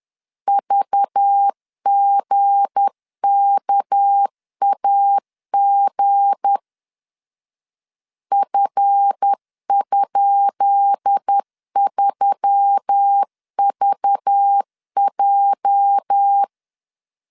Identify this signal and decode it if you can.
Morse code, heard as a keyed tone, about 10 wpm, 800 Hz. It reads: VGKAG F?3VJ
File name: LEZIONI / 10 SIGNS / 10B INT 10wpm 2844